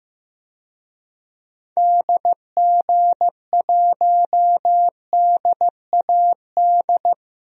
Morse code, heard as a keyed tone, about 16 wpm, 700 Hz. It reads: DG1DAD
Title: DXCC-100 Call Signs - 6 Characters - Repeated 2x - 15wpm